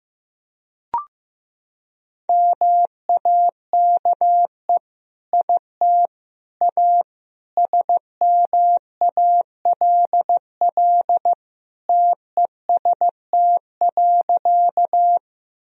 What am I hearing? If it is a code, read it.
Morse code, heard as a keyed tone, about 15 wpm, 700 Hz. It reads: MAKE IT A SMALL TEST.
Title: Sentences from Top 1000 Words - Review Part 3 - 15wpm